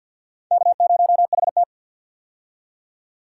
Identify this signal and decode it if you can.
Morse code, heard as a keyed tone, about 51 wpm, 700 Hz. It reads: K0HT
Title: K1USN SST QSO Practice - 50wpm